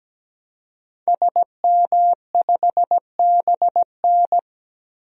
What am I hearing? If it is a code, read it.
Morse code, heard as a keyed tone, about 17 wpm, 700 Hz. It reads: SM5BN
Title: DXCC-100 Call Signs - 5 Characters - 17wpm